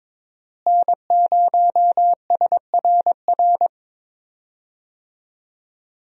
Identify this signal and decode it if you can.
Morse code, heard as a keyed tone, about 22 wpm, 700 Hz. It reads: N0SRR
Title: USA Call Signs - 5 Characters - Repeated 2x - 22wpm